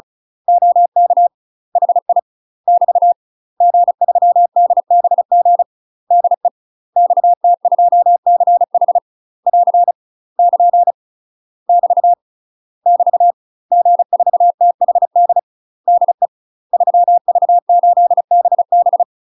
Morse code, heard as a keyed tone, about 36 wpm, 700 Hz. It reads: OK HI <BT> G3DBG DE XT2CH <AR> <KN> <BT> <BT> G4THD DE 3V8BB
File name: UK MASTER QSOs TRAINING / 35WPM / UK MASTER QSOs 0020